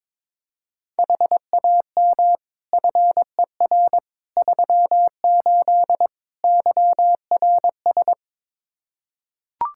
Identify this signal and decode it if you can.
Morse code, heard as a keyed tone, about 22 wpm, 700 Hz. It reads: HAM FER 38 YRS
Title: K1USN SST QSO Practice 22wpm